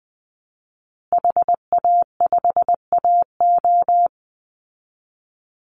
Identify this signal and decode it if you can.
Morse code, heard as a keyed tone, about 20 wpm, 700 Hz. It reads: HA5AO
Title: Call Signs Repeated 3x - 20wpm